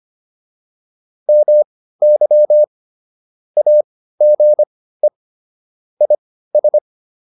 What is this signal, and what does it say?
Morse code, heard as a keyed tone, about 25 wpm, spaced as for 15 wpm, 600 Hz. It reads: MY AGE IS